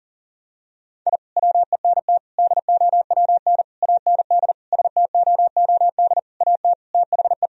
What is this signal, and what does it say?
Morse code, heard as a keyed tone, about 40 wpm, 700 Hz. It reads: I WENT DOWN AND STOOD AT THE
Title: Sentences from Top 1000 Words - Review Part 2 - 40wpm